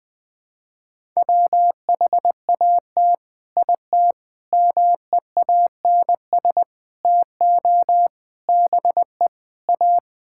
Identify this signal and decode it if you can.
Morse code, heard as a keyed tone, about 21 wpm, 700 Hz. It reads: WHAT IT MEANS TO BE A
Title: Sets of 6 Words - 20wpm